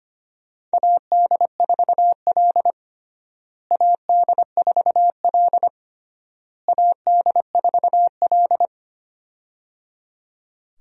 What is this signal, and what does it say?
Morse code, heard as a keyed tone, about 25 wpm, 700 Hz. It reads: AD4L AD4L AD4L
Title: USA Call Signs - 4 Characters - Repeated 3x - 25wpm